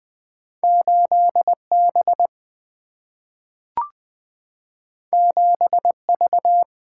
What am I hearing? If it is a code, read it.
Morse code, heard as a keyed tone, about 20 wpm, 700 Hz. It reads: 8B 7V
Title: Single-Letter Single-Number 20wpm